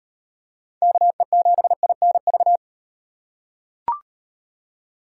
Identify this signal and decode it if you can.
Morse code, heard as a keyed tone, about 37 wpm, 700 Hz. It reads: KE7INV